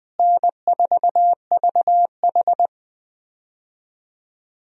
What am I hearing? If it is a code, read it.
Morse code, heard as a keyed tone, about 20 wpm, 700 Hz. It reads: N4VH
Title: USA Call Signs - 4 Characters - Repeated 3x - 20wpm